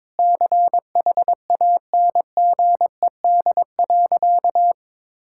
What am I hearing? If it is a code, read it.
Morse code, heard as a keyed tone, about 22 wpm, 700 Hz. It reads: CHANGED.
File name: practice / 22wpm / Sentences with 4-Word Phrases - 22wpm